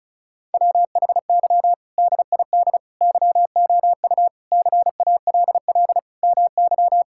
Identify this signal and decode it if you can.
Morse code, heard as a keyed tone, about 36 wpm, 700 Hz. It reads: WHY DID YOU CALL MY
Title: Sentences from Top 1000 Words - Review Part 6 - 35wpm